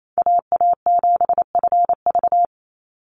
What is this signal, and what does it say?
Morse code, heard as a keyed tone, about 28 wpm, 700 Hz. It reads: AA7FV
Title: Call Signs Repeated 3x - 28wpm